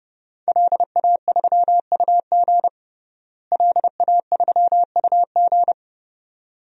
Morse code, heard as a keyed tone, about 32 wpm, 700 Hz. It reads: LA3UG LA3UG
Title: DXCC-100 Call Signs - 5 Characters - Repeated 2x - 30wpm